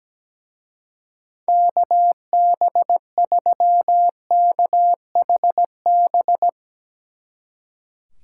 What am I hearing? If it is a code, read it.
Morse code, heard as a keyed tone, about 17 wpm, 700 Hz. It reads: KB3KHB
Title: Speed Racing - USA Call Signs 17wpm